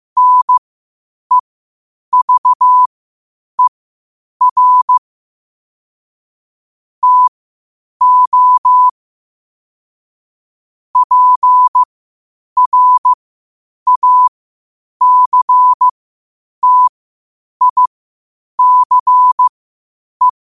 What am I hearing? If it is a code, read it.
Morse code, heard as a keyed tone, about 15 wpm, spaced as for 8 wpm, 1000 Hz. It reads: NEVER TO PRACTICE
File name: qotdfiles / 8wpm